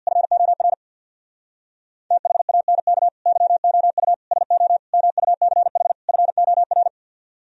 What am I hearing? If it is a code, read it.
Morse code, heard as a keyed tone, about 65 wpm, 700 Hz. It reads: FOR THANK YOU SO MUCH FOR